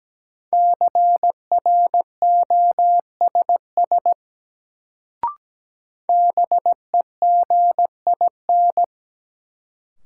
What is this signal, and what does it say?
Morse code, heard as a keyed tone, about 17 wpm, 700 Hz. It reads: CROSS BEGIN
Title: Sentences from Top 300 Words - New Word Drill - 17wpm